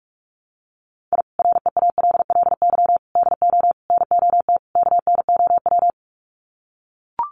Morse code, heard as a keyed tone, about 44 wpm, 700 Hz. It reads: I REALLY DO NOT KNOW
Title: Sets of 5 Words Encore 45wpm